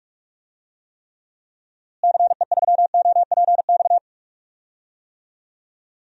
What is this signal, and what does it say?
Morse code, heard as a keyed tone, about 44 wpm, 700 Hz. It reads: CE3OPX
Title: DXCC-100 Call Signs - 6 Characters - Repeated 2x - 45wpm